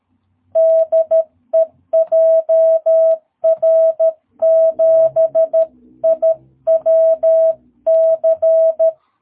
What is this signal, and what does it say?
Morse code, heard as a keyed tone, about 13 wpm, 660 Hz. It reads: DE JR7IWC
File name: jh8cbh / 2022cw